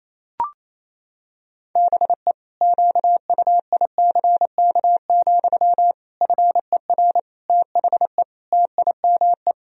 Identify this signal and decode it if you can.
Morse code, heard as a keyed tone, about 28 wpm, 700 Hz. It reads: BE QUICK, FER THE TIME
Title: Sentences from Top 500 Words and Abbreviations 28wpm